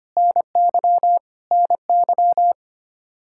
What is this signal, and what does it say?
Morse code, heard as a keyed tone, about 25 wpm, 700 Hz. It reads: NY NY
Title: K1USN SST QSO Practice 25wpm